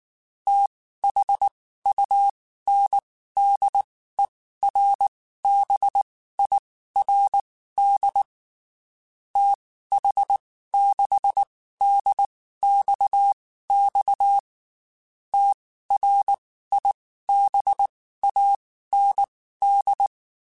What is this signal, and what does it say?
Morse code, heard as a keyed tone, about 19 wpm, spaced as for 14 wpm, 800 Hz. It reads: THUNDERBIRD TH6DXX TRIBAND
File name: QSOs / QSOs-mp3s / WD0BGP 1@18